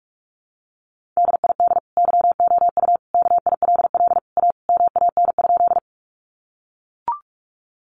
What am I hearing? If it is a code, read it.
Morse code, heard as a keyed tone, about 44 wpm, 700 Hz. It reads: DID YOU KILL A MAN?
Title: Sentences from Top 1400 Words 45wpm